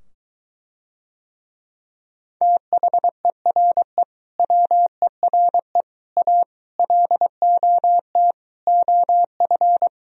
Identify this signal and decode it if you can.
Morse code, heard as a keyed tone, about 23 wpm, 700 Hz. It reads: THERE WERE A LOT OF